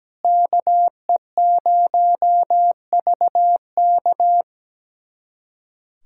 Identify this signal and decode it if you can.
Morse code, heard as a keyed tone, about 17 wpm, 700 Hz. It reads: KE0VK
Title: USA Callsign DE Callsign - 5 Characters - 17wpm